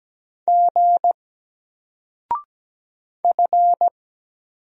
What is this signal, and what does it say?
Morse code, heard as a keyed tone, about 17 wpm, 700 Hz. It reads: G F
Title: Single Letter-Number 17wpm